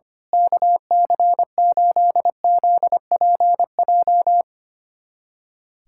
Morse code, heard as a keyed tone, about 25 wpm, 700 Hz. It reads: KC8ZPJ
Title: USA Callsign DE Callsign - 6 Characters - 25wpm